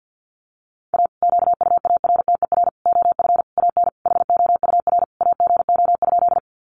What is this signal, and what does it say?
Morse code, heard as a keyed tone, about 50 wpm, 700 Hz. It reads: A QUARTER OF AN HOUR AGO?
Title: Sentences from Top 1300 Words 50wpm